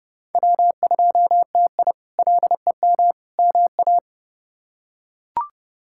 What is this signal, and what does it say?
Morse code, heard as a keyed tone, about 31 wpm, 700 Hz. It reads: W2TI LEM MA
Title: K1USN SST QSO Practice - 30wpm